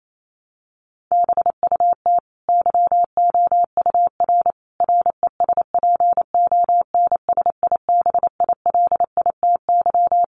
Morse code, heard as a keyed tone, about 28 wpm, 700 Hz. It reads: BUT YOUR RESPONSIBILITY
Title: Sentences from Top 1200 Words 28wpm